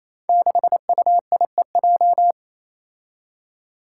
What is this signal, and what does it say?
Morse code, heard as a keyed tone, about 29 wpm, 700 Hz. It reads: 6UIEJ
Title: Five Challenging Letter-Number Combinations 28wpm